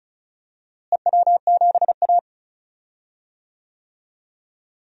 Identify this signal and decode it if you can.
Morse code, heard as a keyed tone, about 34 wpm, 700 Hz. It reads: EW7A
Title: DXCC-100 Call Signs - 4 Characters - Repeated 2x - 35wpm